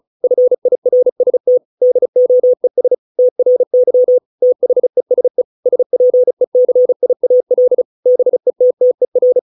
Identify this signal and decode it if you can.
Morse code, heard as a keyed tone, about 36 wpm, 500 Hz. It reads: FIRST DOES TRY THESE SPECIAL BETTER